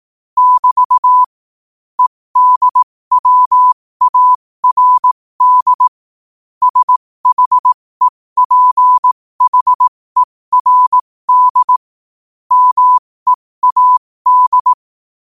Morse code, heard as a keyed tone, about 18 wpm, spaced as for 15 wpm, 1000 Hz. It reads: <BT> EDWARD SHEPHERD MEAD